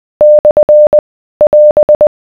Morse code, heard as a keyed tone, about 20 wpm, 600 Hz. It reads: / <AS>